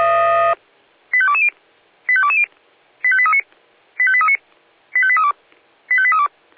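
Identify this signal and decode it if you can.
Five-tone selective call (ZVEI), heard as quick runs of five tones, tones of about 70 ms. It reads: 85220 85220 85839 85839 85832 85832
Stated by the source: ZVEI five-tone call 85270 received by SDR on channel 496.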